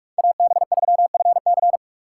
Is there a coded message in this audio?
Yes: Morse code from 0.2 to 1.8 s, heard as a keyed tone, about 44 wpm, 700 Hz. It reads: AB3FC